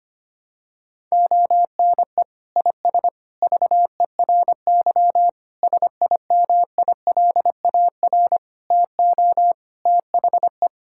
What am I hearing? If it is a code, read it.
Morse code, heard as a keyed tone, about 25 wpm, 700 Hz. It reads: ONE IS VERY SIMILAR TO THE
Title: Sentences from Top 1000 Words - Review Part 5 - 25wpm